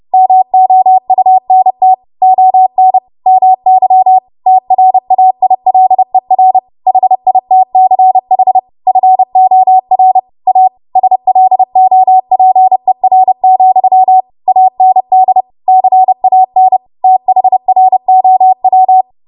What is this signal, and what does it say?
Morse code, heard as a keyed tone, about 30 wpm, 750 Hz. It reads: MOUNT ON MY TRAILER HITCH FOR A SLOPER, AND CAN THROW
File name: Morse / 30WPM